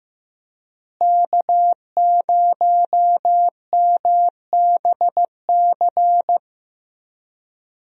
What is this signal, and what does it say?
Morse code, heard as a keyed tone, about 15 wpm, 700 Hz. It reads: K0MBC
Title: USA Call Signs - 5 Characters - Repeated 3x - 15wpm